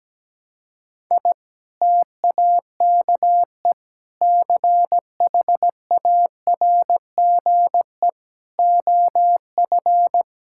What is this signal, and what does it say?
Morse code, heard as a keyed tone, about 17 wpm, 700 Hz. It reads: I TAKE CHARGE OF
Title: Sentences from Top 1000 Words - Review Part 6 - 17wpm